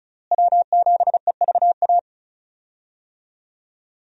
Morse code, heard as a keyed tone, about 34 wpm, 700 Hz. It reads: W7EVA
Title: USA Call Signs - 5 Characters - Repeated 3x - 35wpm